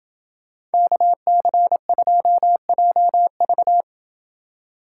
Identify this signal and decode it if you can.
Morse code, heard as a keyed tone, about 29 wpm, 700 Hz. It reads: KC2JV